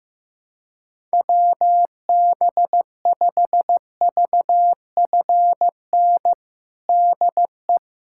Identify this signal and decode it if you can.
Morse code, heard as a keyed tone, about 15 wpm, 700 Hz. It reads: WB5VFN DE
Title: USA Callsign DE Callsign - 6 Characters - 15wpm